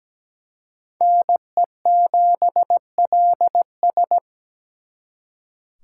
Morse code, heard as a keyed tone, about 20 wpm, 700 Hz. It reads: NE7LS
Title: Speed Racing - USA 2x2 Call Signs 17wpm